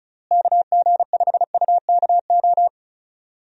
Morse code, heard as a keyed tone, about 36 wpm, 700 Hz. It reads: KG5UKO
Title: USA Call Signs - 6 Characters - Repeated 3x - 35wpm